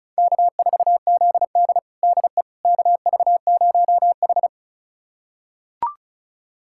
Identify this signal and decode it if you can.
Morse code, heard as a keyed tone, about 34 wpm, 700 Hz. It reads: K4ZD DE KV0H
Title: USA Callsign DE Callsign - 4 Characters - 35wpm